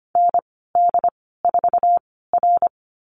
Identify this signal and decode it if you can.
Morse code, heard as a keyed tone, about 25 wpm, spaced as for 16 wpm, 700 Hz. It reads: ND4R
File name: Lesson 15g - Number 5 - Call Signs 25wpm